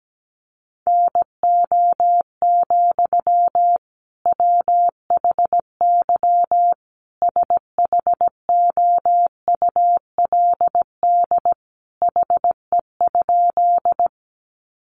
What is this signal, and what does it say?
Morse code, heard as a keyed tone, about 17 wpm, 700 Hz. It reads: NO, WHY SHOULD HE?
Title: Sentences from Top 300 Words 17wpm